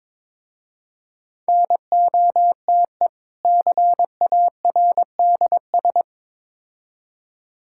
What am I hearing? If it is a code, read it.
Morse code, heard as a keyed tone, about 22 wpm, 700 Hz. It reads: NOTE CARDS
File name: Difficult Words with Hints - 22wpm